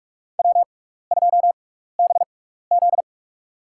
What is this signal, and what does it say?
Morse code, heard as a keyed tone, about 46 wpm, spaced as for 16 wpm, 700 Hz. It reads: W2BZ
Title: USA Call Signs - 4 Characters - 6x Character Spacing - 45wpm